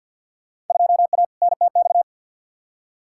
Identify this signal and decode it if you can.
Morse code, heard as a keyed tone, about 52 wpm, 700 Hz. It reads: 2A NTX
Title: Field Day Exchanges - 50wpm